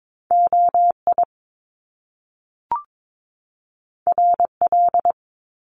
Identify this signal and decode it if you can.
Morse code, heard as a keyed tone, about 22 wpm, 700 Hz. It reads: OI RL
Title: Top Two Letter Combinations in Words 22wpm